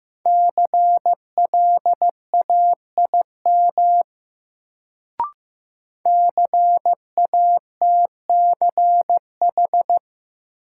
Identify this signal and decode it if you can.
Morse code, heard as a keyed tone, about 15 wpm, 700 Hz. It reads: CLAIM CATCH
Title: Sentences from Top 1500 Words - New Word Drill - 15wpm